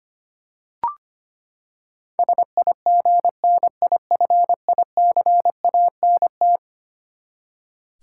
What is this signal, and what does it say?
Morse code, heard as a keyed tone, about 25 wpm, 700 Hz. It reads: SIGNIFICANT
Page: morse-code-ninja-practice-25wpm